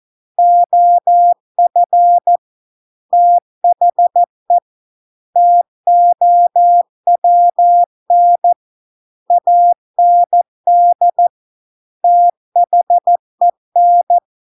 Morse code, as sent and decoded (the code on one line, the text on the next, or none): --- ..-. / - .... / - --- .-- -. / .- -. -.. / - .... -.
OF THE TOWN AND THEN